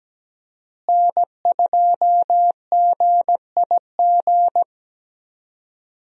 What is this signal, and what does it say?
Morse code, heard as a keyed tone, about 17 wpm, 700 Hz. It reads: N2GIG
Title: USA Call Signs - 5 Characters - Repeated 3x - 17wpm